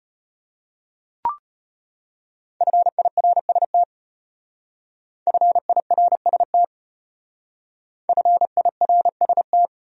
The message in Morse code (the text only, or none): FIRST FIRST FIRST